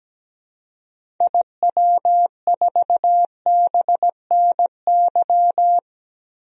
Morse code, heard as a keyed tone, about 17 wpm, 700 Hz. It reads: IW4BNY
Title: DXCC-100 Call Signs - 6 Characters - Repeated 3x - 17wpm